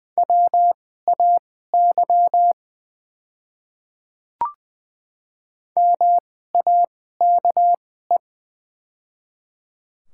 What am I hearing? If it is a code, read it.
Morse code, heard as a keyed tone, about 20 wpm, spaced as for 14 wpm, 690 Hz. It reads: WAY MAKE
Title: 2x Character Spacing - Top 200 Words - 20wpm